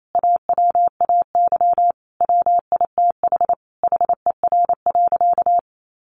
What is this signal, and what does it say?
Morse code, heard as a keyed tone, about 28 wpm, 700 Hz. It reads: AWAY WITH HER.
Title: Sentences from Top 1500 Words 28wpm